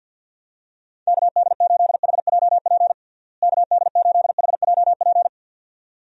Morse code, heard as a keyed tone, about 50 wpm, 700 Hz. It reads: KD8HJP KD8HJP
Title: USA Call Signs - 6 Characters - Repeated 2x - 50wpm